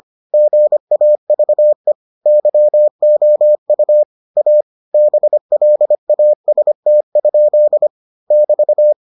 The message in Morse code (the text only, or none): GAVE YOU A BLAST? <BT>